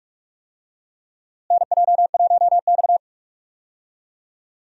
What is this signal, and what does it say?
Morse code, heard as a keyed tone, about 46 wpm, 700 Hz. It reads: NJ1X
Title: USA Call Signs - 4 Characters - Repeated 3x - 45wpm